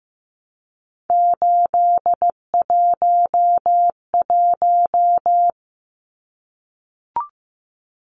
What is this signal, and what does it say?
Morse code, heard as a keyed tone, about 15 wpm, 700 Hz. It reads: 811
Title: Numbers - 3-Digit 15wpm